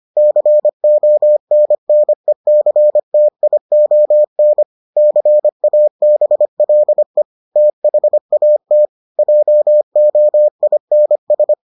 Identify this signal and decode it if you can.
Morse code, heard as a keyed tone, about 25 wpm, 600 Hz. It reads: CONNECTION CABLE THAT JOINS